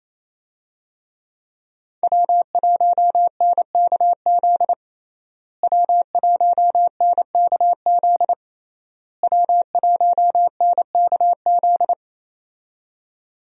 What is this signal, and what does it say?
Morse code, heard as a keyed tone, about 28 wpm, 700 Hz. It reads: W1NKZ W1NKZ W1NKZ
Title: USA Call Signs - 5 Characters - Repeated 3x - 28wpm